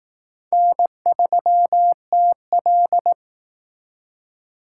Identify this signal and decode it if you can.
Morse code, heard as a keyed tone, about 18 wpm, 700 Hz. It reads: N3TL
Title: Speed Racing - USA 1x2 Call Signs 15wpm